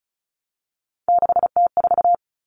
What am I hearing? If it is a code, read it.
Morse code, heard as a keyed tone, about 34 wpm, 700 Hz. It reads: 6T4
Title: Common Cut Numbers - 3-Digit 35wpm